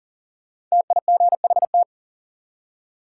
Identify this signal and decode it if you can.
Morse code, heard as a keyed tone, about 40 wpm, 700 Hz. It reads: TIGHT
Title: Five Letter Words - 40wpm